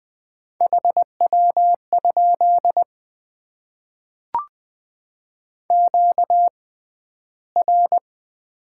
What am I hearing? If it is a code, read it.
Morse code, heard as a keyed tone, about 20 wpm, spaced as for 12 wpm, 700 Hz. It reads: HW? QR
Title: Top 90 QSO Elements - 6x Character Spacing - 20wpm